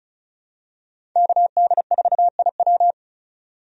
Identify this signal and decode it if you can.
Morse code, heard as a keyed tone, about 36 wpm, 700 Hz. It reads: KD4IW
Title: USA Call Signs - 5 Characters - 35wpm